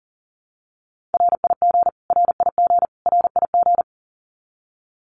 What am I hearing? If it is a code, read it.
Morse code, heard as a keyed tone, about 40 wpm, 700 Hz. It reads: RIG RIG RIG
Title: Abbreviations Repeated 3x - 40wpm